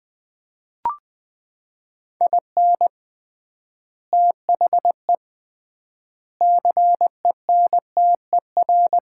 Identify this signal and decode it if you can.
Morse code, heard as a keyed tone, about 21 wpm, 700 Hz. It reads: IN THE CENTER
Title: 3x Word-Spacing - Sets of 5 Words - 20wpm